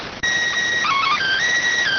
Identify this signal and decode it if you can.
SSTV (slow-scan television), PD 290.